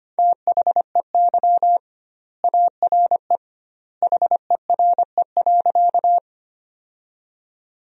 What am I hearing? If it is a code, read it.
Morse code, heard as a keyed tone, about 25 wpm, 700 Hz. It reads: THEY ARE HERE.
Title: 2x Word-Spacing - Sentences from Top 200 Words - 25wpm